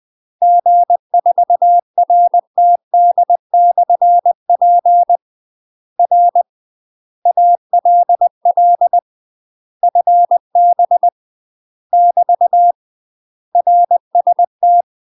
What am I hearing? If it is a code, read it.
Morse code, heard as a keyed tone, about 20 wpm, 700 Hz. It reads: G4RTD/P R ALL FB <BT> RST
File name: UK MASTER QSOs TRAINING / 20WPM / UK MASTER QSOs 0072